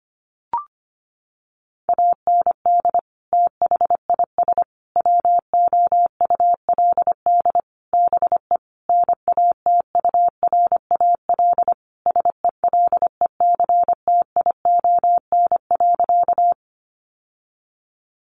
Morse code, heard as a keyed tone, about 25 wpm, 700 Hz. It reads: AND THIS WOULD BE NATURAL SELECTION.
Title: Sentences from Top 1400 Words 25wpm